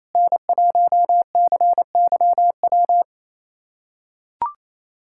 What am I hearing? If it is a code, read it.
Morse code, heard as a keyed tone, about 27 wpm, 690 Hz. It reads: N1CYW
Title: SOTA QSO Practice 28wpm